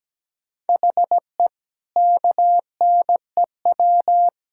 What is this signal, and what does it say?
Morse code, heard as a keyed tone, about 17 wpm, 700 Hz. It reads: HE KNEW
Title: Sentences from Top 1000 Words - Review Part 5 - 17wpm